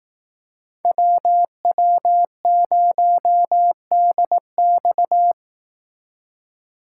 Morse code, heard as a keyed tone, about 18 wpm, 700 Hz. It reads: WW0DX
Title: Speed Racing - USA 2x2 Call Signs 15wpm